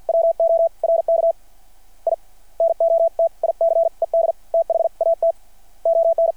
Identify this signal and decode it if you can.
Morse code, heard as a keyed tone, about 52 wpm, spaced as for 43 wpm, 660 Hz. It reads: WORK I NOTICED THAT ON
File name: archived / CRSnet